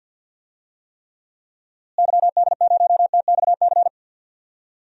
Morse code, heard as a keyed tone, about 51 wpm, 700 Hz. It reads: YD0TXC
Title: DXCC-100 Call Signs - 6 Characters - Repeated 3x - 50wpm